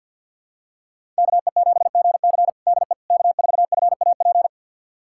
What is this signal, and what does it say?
Morse code, heard as a keyed tone, about 46 wpm, 700 Hz. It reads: KE7GC DE K4FAP
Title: USA Callsign DE Callsign 50wpm